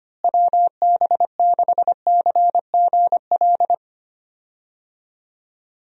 Morse code, heard as a keyed tone, about 25 wpm, 700 Hz. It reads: WB6CGL
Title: USA Call Signs - 6 Characters - Repeated 2x - 25wpm